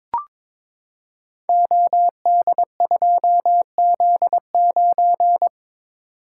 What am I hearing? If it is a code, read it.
Morse code, heard as a keyed tone, about 23 wpm, 700 Hz. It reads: OD2Z9
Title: Code Groups - 5 Letter:Numbers - 22wpm